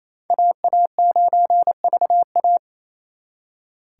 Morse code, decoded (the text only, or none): AA9VA